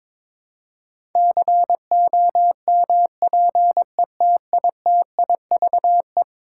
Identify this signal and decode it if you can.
Morse code, heard as a keyed tone, about 22 wpm, 700 Hz. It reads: COMPETITIVE
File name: practice / 22wpm / Eleven Letter Words - 22wpm